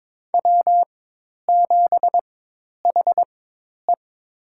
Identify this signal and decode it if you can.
Morse code, heard as a keyed tone, about 22 wpm, spaced as for 10 wpm, 700 Hz. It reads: W7HE
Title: USA Call Signs - 4 Characters - 4x Character Spacing - 22wpm